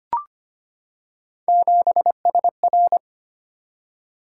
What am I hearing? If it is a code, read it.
Morse code, heard as a keyed tone, about 25 wpm, 700 Hz. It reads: 7SR
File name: Three Letter-Number Combinations - 25wpm